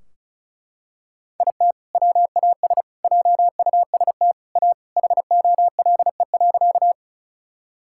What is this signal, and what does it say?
Morse code, heard as a keyed tone, about 36 wpm, 700 Hz. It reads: IT WAS JUST A HOLE.
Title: Sentences from Top 1600 Words 35wpm